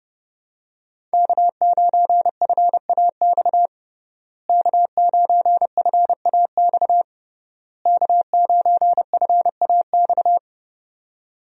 Morse code, heard as a keyed tone, about 30 wpm, 700 Hz. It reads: K9FAX K9FAX K9FAX
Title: USA Call Signs - 5 Characters - Repeated 3x - 30wpm